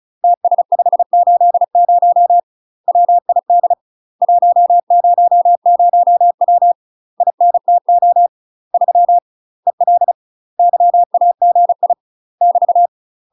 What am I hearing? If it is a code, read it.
Morse code, heard as a keyed tone, about 36 wpm, 700 Hz. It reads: TS580 WID 100W INTO 3 EL YAGI <BT>